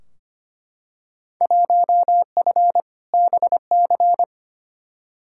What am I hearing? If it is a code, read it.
Morse code, heard as a keyed tone, about 25 wpm, 700 Hz. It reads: 1F BC
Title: Field Day Exchanges Encore - 25wpm